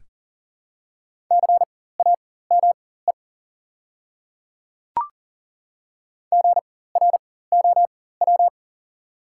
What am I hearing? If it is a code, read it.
Morse code, heard as a keyed tone, about 40 wpm, spaced as for 19 wpm, 700 Hz. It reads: CAME GROW
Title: 4x Character Spacing - Top 300 Words - 40wpm